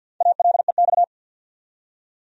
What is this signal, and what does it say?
Morse code, heard as a keyed tone, about 52 wpm, 700 Hz. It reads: ALEX
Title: USA Common Names - 50wpm